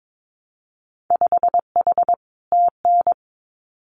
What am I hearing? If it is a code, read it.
Morse code, heard as a keyed tone, about 22 wpm, 700 Hz. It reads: 5H TN
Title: Winter Field Day Exchanges Encore 22wpm